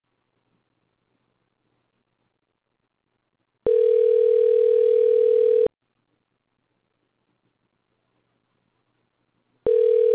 Ringback tone (North America), 440 + 480 Hz.